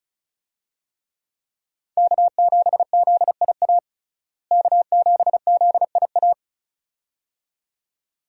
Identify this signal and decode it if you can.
Morse code, heard as a keyed tone, about 36 wpm, 700 Hz. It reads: K7ZIA K7ZIA
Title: USA Call Signs - 5 Characters - Repeated 2x - 35wpm